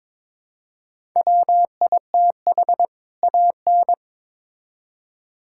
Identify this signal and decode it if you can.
Morse code, heard as a keyed tone, about 22 wpm, 700 Hz. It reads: WITH AN
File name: Build A Phrase - 2-Word Phrases - 22wpm